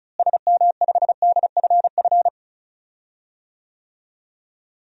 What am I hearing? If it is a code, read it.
Morse code, heard as a keyed tone, about 34 wpm, 700 Hz. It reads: SM5DFF
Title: DXCC-100 Call Signs - 6 Characters - Repeated 2x - 35wpm